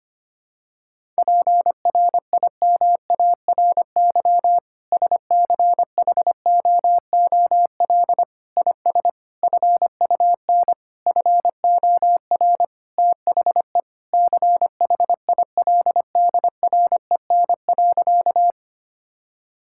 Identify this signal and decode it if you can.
Morse code, heard as a keyed tone, about 25 wpm, 700 Hz. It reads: PRIMARY SCHOOL IS FUN FOR THE CHILDREN.
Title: Sentences from Top 1000 Words - Review Part 5 - 25wpm